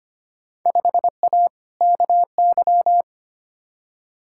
Morse code, heard as a keyed tone, about 25 wpm, 700 Hz. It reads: 5A KY
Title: Field Day Exchanges - 25wpm